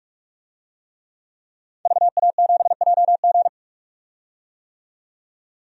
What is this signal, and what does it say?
Morse code, heard as a keyed tone, about 46 wpm, 700 Hz. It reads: VA7JG
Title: DXCC-100 Call Signs - 5 Characters - 45wpm